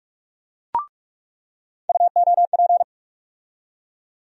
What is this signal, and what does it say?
Morse code, heard as a keyed tone, about 42 wpm, 700 Hz. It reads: UOP
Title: USA Call Sign Postfixes - 3 Letters 45wpm